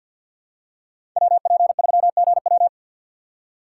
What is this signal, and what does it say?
Morse code, heard as a keyed tone, about 52 wpm, 700 Hz. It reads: WP3GW
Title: USA Call Signs - 5 Characters - 6x Character Spacing - 50wpm